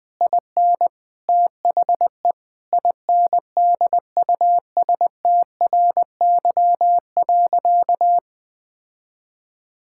Morse code, heard as a keyed tone, about 21 wpm, 700 Hz. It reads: IN THE INDUSTRY.
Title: Sentences with 8-Word Phrases - 20wpm